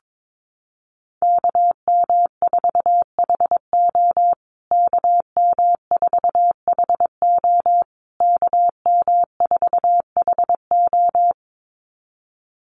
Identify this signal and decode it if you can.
Morse code, heard as a keyed tone, about 22 wpm, 700 Hz. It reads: KM4HO KM4HO KM4HO
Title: USA Call Signs Repeated 3x 22wpm